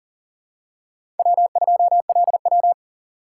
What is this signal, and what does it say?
Morse code, heard as a keyed tone, about 40 wpm, 700 Hz. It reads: W2LW
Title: USA Call Signs - 4 Characters - 6x Character Spacing - 40wpm